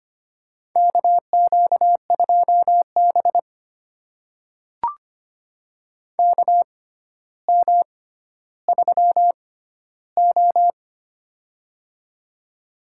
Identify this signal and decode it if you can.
Morse code, heard as a keyed tone, about 25 wpm, spaced as for 13 wpm, 700 Hz. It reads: KQ2B KM3O
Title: USA Call Signs - 4 Characters - 6x Character Spacing - 25wpm